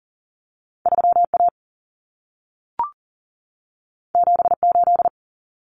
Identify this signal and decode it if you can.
Morse code, heard as a keyed tone, about 40 wpm, 700 Hz. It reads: 3A 78
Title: Common Cut Numbers - 2-Digit 40wpm